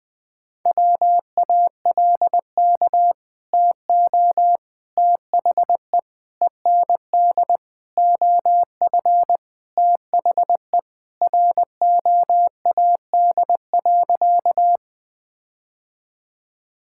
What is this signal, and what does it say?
Morse code, heard as a keyed tone, about 21 wpm, 700 Hz. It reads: WALK TO THE END OF THE ROAD.
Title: Sentences with 4-Word Phrases - Spoken Beginning - 20wpm